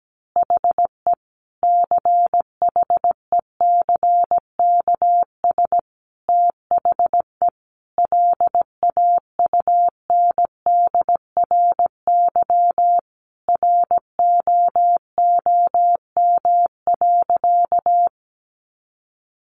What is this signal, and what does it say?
Morse code, heard as a keyed tone, about 17 wpm, 700 Hz. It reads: HE CHECKS THE LAUNDRY ROOM.
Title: Easy Phrases 17wpm